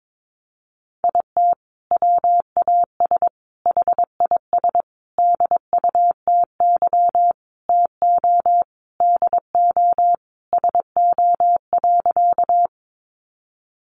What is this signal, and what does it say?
Morse code, heard as a keyed tone, about 22 wpm, 700 Hz. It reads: IT WAS HIS DUTY TO DO SO.
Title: Sentences from Top 1500 Words 22wpm